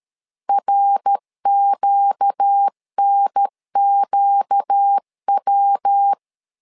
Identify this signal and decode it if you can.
Morse code, heard as a keyed tone, about 13 wpm, 800 Hz. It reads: RQNQW
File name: LEZIONI / 4 Q 12wpm 1833